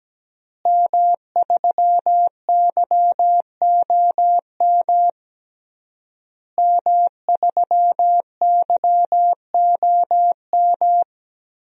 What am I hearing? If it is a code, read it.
Morse code, heard as a keyed tone, about 17 wpm, 700 Hz. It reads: M3YOM M3YOM
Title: DXCC-100 Call Signs - 5 Characters - Repeated 3x - 17wpm